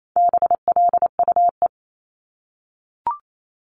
Morse code, heard as a keyed tone, about 28 wpm, 700 Hz. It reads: BLUE
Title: Multi-letter Phonemes 28wpm